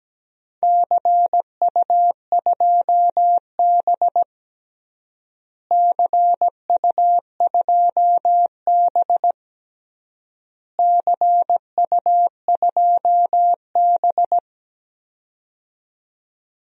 Morse code, heard as a keyed tone, about 17 wpm, 700 Hz. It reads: CU2B CU2B CU2B
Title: DXCC-100 Call Signs - 4 Characters - Repeated 3x - 17wpm